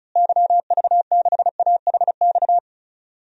Morse code, heard as a keyed tone, about 34 wpm, 700 Hz. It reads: YV6AHX
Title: DXCC-100 Call Signs - 6 Characters - Repeated 3x - 35wpm